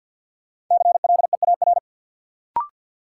Morse code, heard as a keyed tone, about 51 wpm, 700 Hz. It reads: CLEAR
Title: Sentences from Top 500 Words - New Word Drill - 50wpm